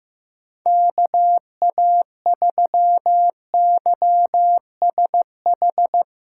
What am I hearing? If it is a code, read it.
Morse code, heard as a keyed tone, about 16 wpm, 700 Hz. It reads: KA3YSH
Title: USA Call Signs - 6x Character Spacing - 15wpm